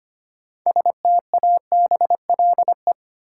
Morse code, heard as a keyed tone, about 25 wpm, 700 Hz. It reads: STABLE
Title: Sentences from Top 1500 Words - New Word Drill Alt - 25wpm